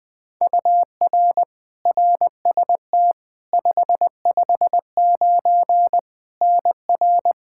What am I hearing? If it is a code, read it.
Morse code, heard as a keyed tone, about 20 wpm, 700 Hz. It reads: UR RST 559 NR
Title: K1USN SST QSO Practice 20wpm